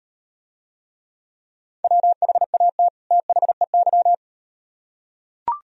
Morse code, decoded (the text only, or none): WHAT THEY